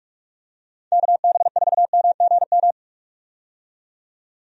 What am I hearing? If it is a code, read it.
Morse code, heard as a keyed tone, about 45 wpm, 700 Hz. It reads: KB4MGM
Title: USA Call Signs - 6 Characters - 45wpm